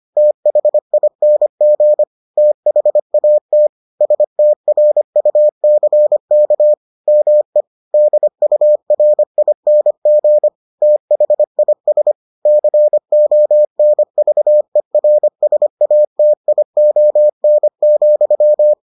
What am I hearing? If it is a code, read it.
Morse code, heard as a keyed tone, about 25 wpm, 600 Hz. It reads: THING THAT STRUCK ME DURING THIS CONVERSATION,